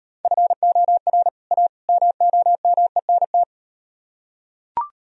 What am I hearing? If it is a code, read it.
Morse code, heard as a keyed tone, about 37 wpm, 700 Hz. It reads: FOR A MOMENT